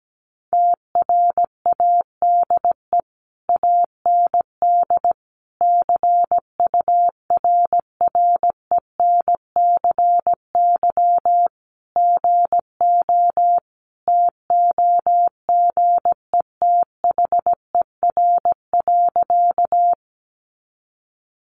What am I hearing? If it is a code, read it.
Morse code, heard as a keyed tone, about 17 wpm, 700 Hz. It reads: TRADE AND CURRENCY GO TOGETHER.
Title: Sentences from Top 1500 Words 17wpm